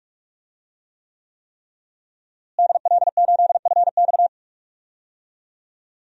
Morse code, heard as a keyed tone, about 46 wpm, 700 Hz. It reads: DL8FX
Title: DXCC-100 Call Signs - 5 Characters - Repeated 3x - 45wpm